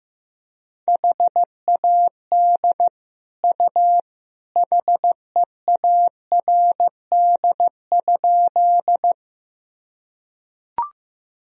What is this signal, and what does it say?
Morse code, heard as a keyed tone, about 15 wpm, 700 Hz. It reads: HAD U HEARD?
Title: Sentences from Top 400 Words and Abbreviations 15wpm